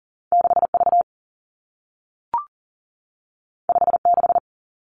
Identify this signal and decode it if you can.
Morse code, heard as a keyed tone, about 39 wpm, 700 Hz. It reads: 6 V 5 6
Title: Two Challenging Letter-Number Combinations 40wpm